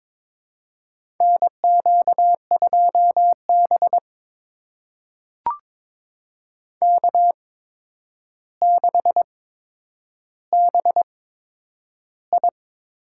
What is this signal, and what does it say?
Morse code, heard as a keyed tone, about 22 wpm, spaced as for 9 wpm, 700 Hz. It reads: NQ2B K6BI
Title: USA Call Signs - 4 Characters - 8x Character Spacing - 22wpm